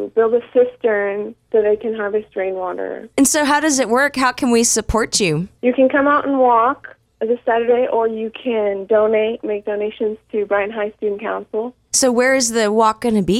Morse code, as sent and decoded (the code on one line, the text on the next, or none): none